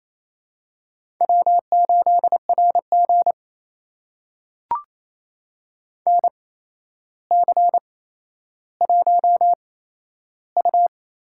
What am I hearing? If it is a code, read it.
Morse code, heard as a keyed tone, about 28 wpm, spaced as for 12 wpm, 700 Hz. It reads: W8RG NC1U
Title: USA Call Signs - 4 Characters - 8x Character Spacing - 28wpm